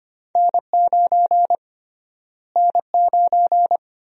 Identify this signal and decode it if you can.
Morse code, heard as a keyed tone, about 25 wpm, 700 Hz. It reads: N9 N9
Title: USA Call Sign Prefixes - Repeated 2x - 25wpm